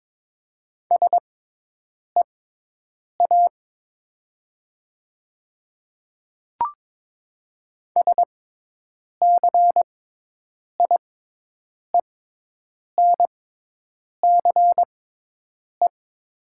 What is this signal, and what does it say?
Morse code, heard as a keyed tone, about 22 wpm, spaced as for 8 wpm, 690 Hz. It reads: SEA SCIENCE
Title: 6x Character Spacing - Top 500 Words - 22wpm